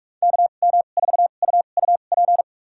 Morse code, heard as a keyed tone, about 46 wpm, spaced as for 32 wpm, 700 Hz. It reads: KM4UUP
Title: USA Call Signs - 6 Characters - 2x Character Spacing - 45wpm